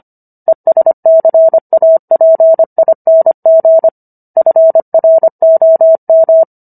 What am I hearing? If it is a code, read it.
Morse code, heard as a keyed tone, about 25 wpm, 650 Hz. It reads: ESCAPING FROM